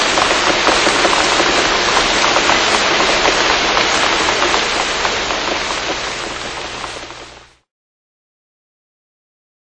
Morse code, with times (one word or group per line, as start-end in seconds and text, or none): none